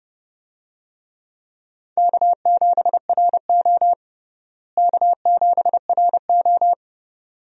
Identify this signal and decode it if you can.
Morse code, heard as a keyed tone, about 32 wpm, 700 Hz. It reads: K7RO K7RO
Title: USA Call Signs - 4 Characters - Repeated 2x - 30wpm